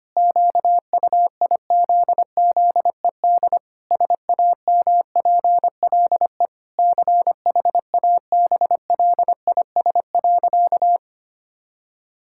Morse code, heard as a keyed tone, about 25 wpm, 700 Hz. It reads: QUIZZED SAMPLE CHABLIS.
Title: Pangram Challenge v2 - 25wpm